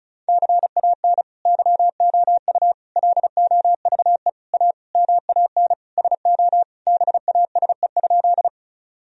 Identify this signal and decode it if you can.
Morse code, heard as a keyed tone, about 36 wpm, 700 Hz. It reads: CAN YOU LOVE A MAN SO BASE?
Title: Sentences from Top 1000 Words - Review Part 1 - 35wpm